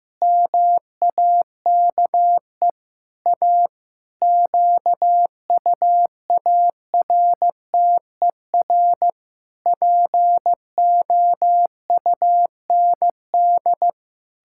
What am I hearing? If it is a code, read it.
Morse code, heard as a keyed tone, about 15 wpm, 700 Hz. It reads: MAKE A QUARTER POUND
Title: Pangram Challenge v2 - 15wpm